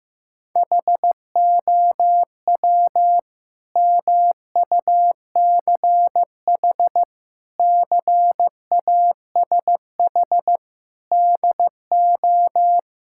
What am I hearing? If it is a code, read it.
Morse code, heard as a keyed tone, about 15 wpm, 700 Hz. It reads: HOW MUCH CASH DO
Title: Sentences from Top 1000 Words - Review Part 5 - 15wpm